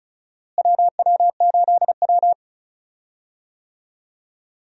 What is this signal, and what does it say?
Morse code, heard as a keyed tone, about 34 wpm, 700 Hz. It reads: WW8W
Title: USA Call Signs - 4 Characters - Repeated 3x - 35wpm